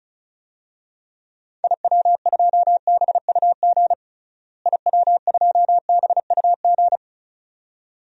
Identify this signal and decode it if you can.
Morse code, heard as a keyed tone, about 36 wpm, 700 Hz. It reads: IW2BUG IW2BUG
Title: DXCC-100 Call Signs - 6 Characters - Repeated 2x - 35wpm